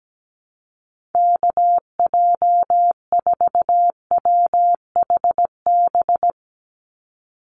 Morse code, heard as a keyed tone, about 17 wpm, 700 Hz. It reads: KJ4WHB
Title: USA Call Signs 17wpm